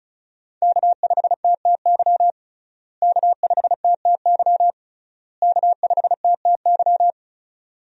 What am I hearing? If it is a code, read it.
Morse code, heard as a keyed tone, about 36 wpm, 700 Hz. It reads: K5TTY K5TTY K5TTY
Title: USA Call Signs - 5 Characters - Repeated 3x - 35wpm